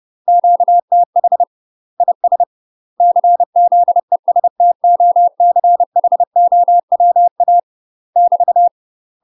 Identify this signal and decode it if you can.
Morse code, heard as a keyed tone, about 30 wpm, 700 Hz. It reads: QTH IS CZESTOCHOWA <BT>
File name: UK MASTER QSOs TRAINING / 30WPM / UK MASTER QSOs 0048